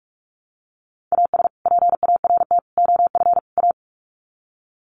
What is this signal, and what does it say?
Morse code, heard as a keyed tone, about 44 wpm, 700 Hz. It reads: AS PART OF A
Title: Sets of 4 Words 45wpm